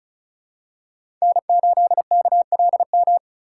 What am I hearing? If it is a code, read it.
Morse code, heard as a keyed tone, about 36 wpm, 700 Hz. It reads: N8KLM
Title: USA Call Signs - 6x Character Spacing - 35wpm